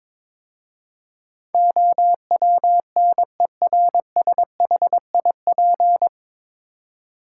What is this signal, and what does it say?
Morse code, heard as a keyed tone, about 22 wpm, 700 Hz. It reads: OWNERSHIP
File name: 9 Letter Words - First 3 Letters Given - 22wpm